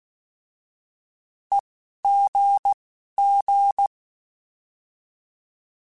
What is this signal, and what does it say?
Morse code, heard as a keyed tone, about 16 wpm, spaced as for 11 wpm, 800 Hz. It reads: EGG
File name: Alphabetical / Alphabetical mp3s / Es @10